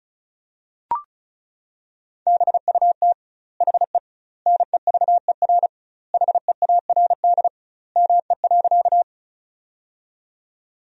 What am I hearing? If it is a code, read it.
Morse code, heard as a keyed tone, about 36 wpm, 700 Hz. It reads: BUT HE NEVER HEARD ME.
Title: 2x Word-Spacing - Sentences from Top 400 Words - 35wpm